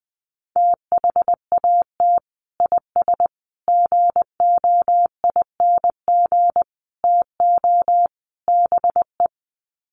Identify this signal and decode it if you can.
Morse code, heard as a keyed tone, about 19 wpm, 700 Hz. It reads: THAT IS GOING TO BE
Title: Sets of 5 Words Encore2 20wpm